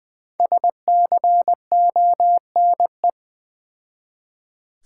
Morse code, heard as a keyed tone, about 20 wpm, 690 Hz. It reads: SCONE
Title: Difficult Words with Hints - 20wpm